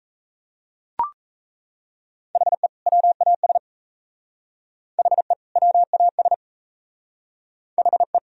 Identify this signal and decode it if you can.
Morse code, heard as a keyed tone, about 34 wpm, 700 Hz. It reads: HE WAS HE WAS HE
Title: Speed-Racing - Sets of 2 Words 28wpm